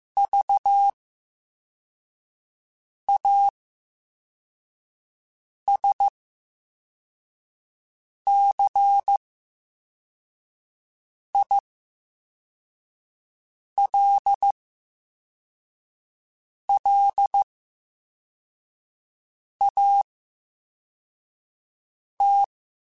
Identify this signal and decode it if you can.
Morse code, heard as a keyed tone, about 15 wpm, spaced as for 4 wpm, 800 Hz. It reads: VASCILLAT
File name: Special INGs / Special INGs-mp3s / Slow ing @13